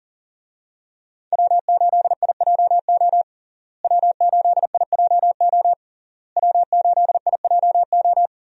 Morse code, heard as a keyed tone, about 37 wpm, 700 Hz. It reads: W8IJO W8IJO W8IJO
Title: USA Call Signs - 5 Characters - Repeated 3x - 40wpm